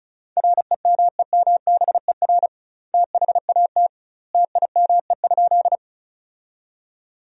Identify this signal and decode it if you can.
Morse code, heard as a keyed tone, about 34 wpm, 700 Hz. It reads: REMEMBER THAT TIME?
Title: 2x Word-Spacing - Sentences from Top 400 Words - 35wpm